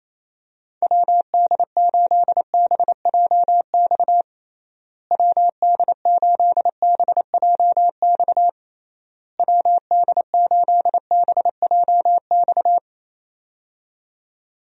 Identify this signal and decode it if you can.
Morse code, heard as a keyed tone, about 28 wpm, 700 Hz. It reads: WD8BJX WD8BJX WD8BJX
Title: USA Call Signs - 6 Characters - Repeated 3x - 28wpm